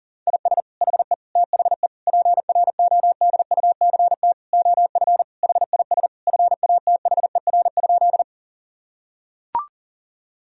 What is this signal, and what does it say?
Morse code, heard as a keyed tone, about 40 wpm, 700 Hz. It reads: IS HE THE PRODUCT OF HIS FATHER?
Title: Sentences from Top 1000 Words - Review Part 2 - 40wpm